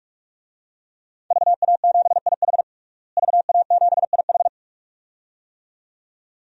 Morse code, heard as a keyed tone, about 46 wpm, 700 Hz. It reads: VA7IH VA7IH
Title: DXCC-100 Call Signs - 5 Characters - Repeated 2x - 45wpm